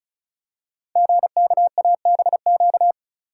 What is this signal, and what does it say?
Morse code, heard as a keyed tone, about 36 wpm, 690 Hz. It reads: GKABQ
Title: Code Groups - 5 Letters - 35wpm